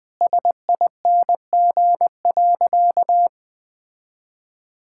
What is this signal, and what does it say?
Morse code, heard as a keyed tone, about 21 wpm, 700 Hz. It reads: SING.
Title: Sentences from Top 1600 Words 20wpm